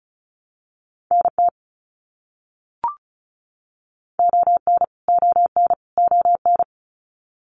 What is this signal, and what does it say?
Morse code, heard as a keyed tone, about 36 wpm, 700 Hz. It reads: NT ON ON ON
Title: Canadian Provinces and Territories - Repeated 3x - 35wpm